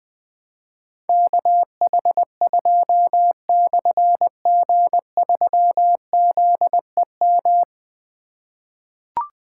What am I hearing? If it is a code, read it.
Morse code, heard as a keyed tone, about 20 wpm, 700 Hz. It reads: KH2/G3ZEM
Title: Difficult Call Signs 20wpm